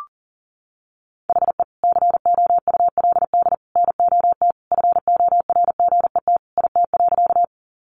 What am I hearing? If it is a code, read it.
Morse code, heard as a keyed tone, about 40 wpm, 700 Hz. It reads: HE COULD NOT FORGET IT.
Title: Sentences from Top 1200 Words 40wpm